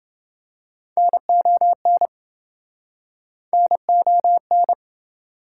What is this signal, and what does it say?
Morse code, heard as a keyed tone, about 29 wpm, 700 Hz. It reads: NON NON